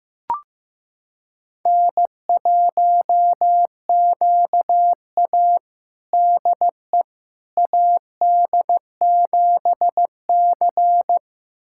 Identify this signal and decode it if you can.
Morse code, heard as a keyed tone, about 15 wpm, 700 Hz. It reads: N1QA DE AD7C
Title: USA Callsign DE Callsign - 4 Characters - 15wpm